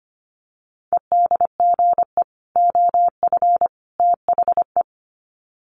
Morse code, heard as a keyed tone, about 25 wpm, 700 Hz. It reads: EDGE OF THE
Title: Sets of 3 Words 25wpm